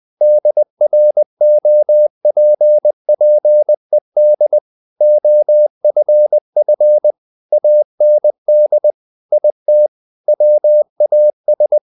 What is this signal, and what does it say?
Morse code, heard as a keyed tone, about 20 wpm, 600 Hz. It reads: DROPPED OFF AND IT WAS
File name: the old man and the sea / the old man and the sea part 1 20wpm